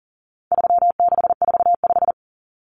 Morse code, heard as a keyed tone, about 40 wpm, 700 Hz. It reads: 3645
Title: Common Cut Numbers - 4-Digit 40wpm